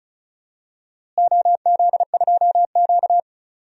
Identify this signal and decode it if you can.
Morse code, heard as a keyed tone, about 34 wpm, 700 Hz. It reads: OZ2Q
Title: DXCC-100 Call Signs - 4 Characters - Repeated 3x - 35wpm